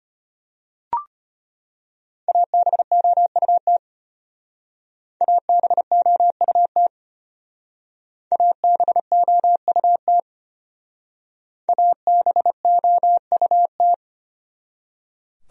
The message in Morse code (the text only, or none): ABOUT ABOUT ABOUT ABOUT